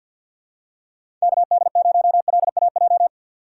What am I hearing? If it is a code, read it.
Morse code, heard as a keyed tone, about 51 wpm, 700 Hz. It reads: KD0LAJ
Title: USA Call Signs - 3x Character Spacing - 50wpm